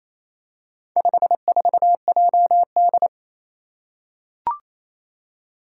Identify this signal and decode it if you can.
Morse code, heard as a keyed tone, about 28 wpm, 700 Hz. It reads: 54JD
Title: Four Challenging Letter-Number Combinations 28wpm